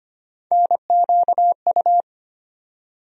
Morse code, heard as a keyed tone, about 25 wpm, 700 Hz. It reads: NQU
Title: USA Call Sign Postfixes - 3 Letters 25wpm